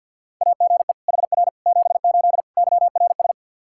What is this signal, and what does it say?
Morse code, heard as a keyed tone, about 48 wpm, 700 Hz. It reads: AGE HR 78 YRS
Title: K1USN SST QSO Practice 50wpm